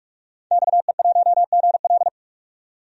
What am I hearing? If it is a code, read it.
Morse code, heard as a keyed tone, about 44 wpm, 700 Hz. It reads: XE1GL
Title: DXCC-100 Call Signs - 5 Characters - 45wpm